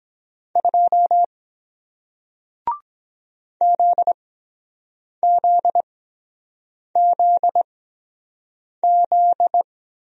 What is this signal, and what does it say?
Morse code, heard as a keyed tone, about 23 wpm, 700 Hz. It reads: 2 Z Z Z Z